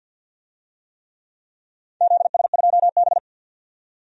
Morse code, heard as a keyed tone, about 50 wpm, 700 Hz. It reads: ZS2B
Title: DXCC-100 Call Signs - 4 Characters - 50wpm